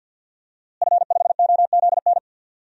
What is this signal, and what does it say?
Morse code, heard as a keyed tone, about 52 wpm, 700 Hz. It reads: F5OZN
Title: DXCC-100 Call Signs - 5 Characters - Repeated 3x - 50wpm